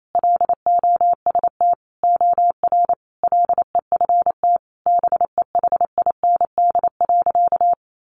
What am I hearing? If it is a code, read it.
Morse code, heard as a keyed tone, about 28 wpm, 700 Hz. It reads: LOST OR LEFT BEHIND.
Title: Sentences from Top 1400 Words 28wpm